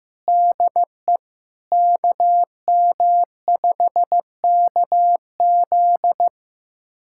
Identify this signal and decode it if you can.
Morse code, heard as a keyed tone, about 16 wpm, 700 Hz. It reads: DE KM5KZ
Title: USA Callsign DE Callsign - 5 Characters - 15wpm